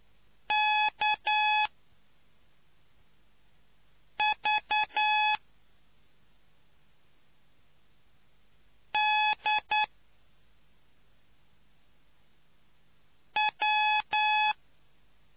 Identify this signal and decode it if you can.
Morse code, heard as a keyed tone, about 9 wpm, 850 Hz. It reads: K V D W